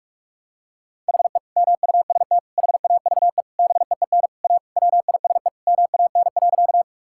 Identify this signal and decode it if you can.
Morse code, heard as a keyed tone, about 46 wpm, 700 Hz. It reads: HE MUST HAVE BEEN A WISE MAN.
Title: Sentences from Top 1600 Words 45wpm